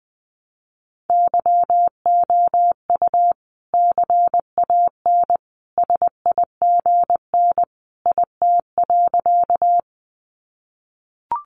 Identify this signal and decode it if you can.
Morse code, heard as a keyed tone, about 19 wpm, 700 Hz. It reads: YOU CAN SIGN IT.
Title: Sentences from Top 1000 Words 20wpm